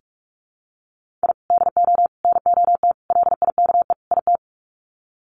Morse code, heard as a keyed tone, about 46 wpm, 700 Hz. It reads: I DO NOT LIKE IT
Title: Sets of 5 Words Encore 45wpm